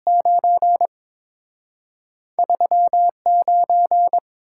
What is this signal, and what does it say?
Morse code, heard as a keyed tone, about 28 wpm, 700 Hz. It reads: 9 39